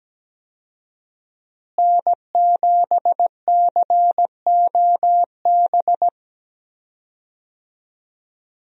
Morse code, heard as a keyed tone, about 17 wpm, 700 Hz. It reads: N7COB
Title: USA Call Signs - 5 Characters - Repeated 3x - 17wpm